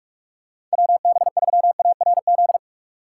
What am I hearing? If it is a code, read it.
Morse code, heard as a keyed tone, about 41 wpm, 700 Hz. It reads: WB3ARZ
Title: USA Call Signs - 6 Characters - 4x Character Spacing - 45wpm